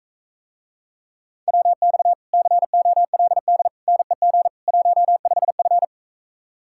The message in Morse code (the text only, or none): WX COLD NEG 15F